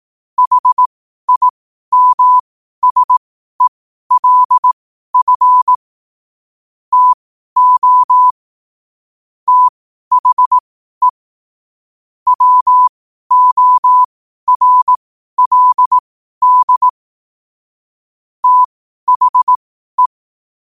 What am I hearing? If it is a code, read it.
Morse code, heard as a keyed tone, about 18 wpm, spaced as for 12 wpm, 1000 Hz. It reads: HIMSELF TO THE WORLD THE
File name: qotdfiles / 12wpm